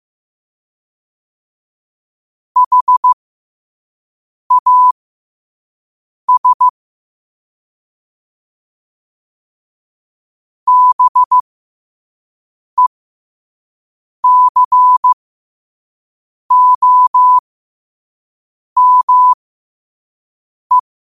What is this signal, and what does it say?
Morse code, heard as a keyed tone, about 15 wpm, spaced as for 5 wpm, 1000 Hz. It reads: HAS BECOME